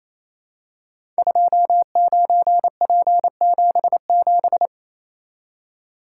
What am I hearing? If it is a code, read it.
Morse code, heard as a keyed tone, about 27 wpm, 700 Hz. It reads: 29P77
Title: Code Groups - 5 Letter:Numbers - 28wpm